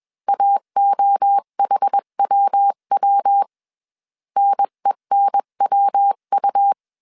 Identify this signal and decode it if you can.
Morse code, heard as a keyed tone, about 21 wpm, 800 Hz. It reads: AOHWW NENWU
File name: CW / DET / 1 3 1500C S B 20wpm 1754